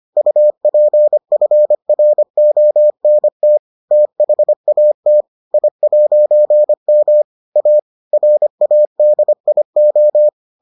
Morse code, heard as a keyed tone, about 25 wpm, 600 Hz. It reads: UPFRONT THAT I'M A RADIO